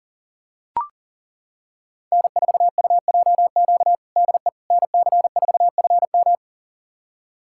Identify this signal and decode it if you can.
Morse code, heard as a keyed tone, about 40 wpm, 700 Hz. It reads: N4UJQ DE NC4FM
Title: USA Callsign DE Callsign - 5 Characters - 40wpm